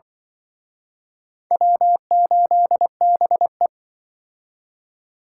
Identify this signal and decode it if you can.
Morse code, heard as a keyed tone, about 24 wpm, 700 Hz. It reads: W8BE
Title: Speed Racing - USA Call Signs 20wpm